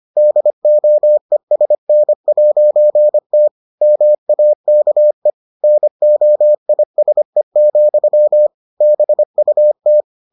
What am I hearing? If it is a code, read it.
Morse code, heard as a keyed tone, about 25 wpm, 600 Hz. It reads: DOESN'T MAKE NOISE, BUT